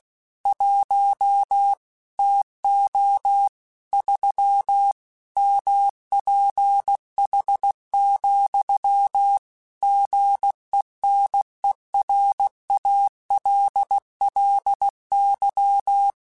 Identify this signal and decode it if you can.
Morse code, heard as a keyed tone, about 16 wpm, 800 Hz. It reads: 1 TO 3 MPH, GENERALLY
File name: Weather Sets / Weather Sets-mp3s / Weather 5 1 @15